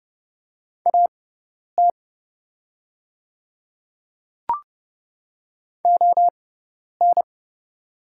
Morse code, heard as a keyed tone, about 31 wpm, spaced as for 10 wpm, 700 Hz. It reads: AT ON
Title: 6x Character Spacing - Top 100 Words - 30wpm